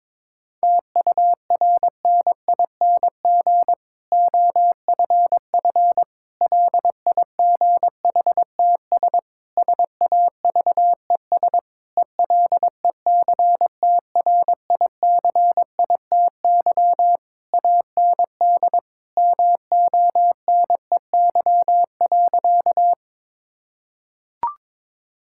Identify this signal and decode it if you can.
Morse code, heard as a keyed tone, about 22 wpm, 700 Hz. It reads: TURNING OFF LIGHTS SAVES ELECTRICITY AND MONEY.
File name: Difficult Sentences with Hints - 22wpm